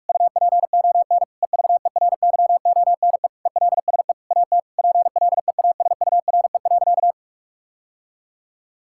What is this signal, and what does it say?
Morse code, heard as a keyed tone, about 41 wpm, 700 Hz. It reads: UPON EVERYONE ELSE AT PLEASURE.
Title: Sentences from Top 1600 Words 45wpm